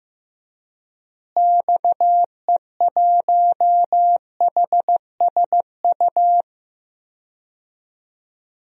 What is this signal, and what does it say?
Morse code, heard as a keyed tone, about 15 wpm, 700 Hz. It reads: XE1HSU
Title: DXCC-100 Call Signs - 6 Characters - Repeated 3x - 15wpm